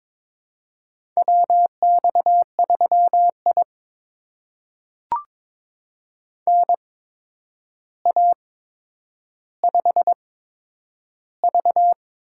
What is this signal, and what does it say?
Morse code, heard as a keyed tone, about 22 wpm, spaced as for 9 wpm, 700 Hz. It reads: WX3I NA5V
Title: USA Call Signs - 4 Characters - 8x Character Spacing - 22wpm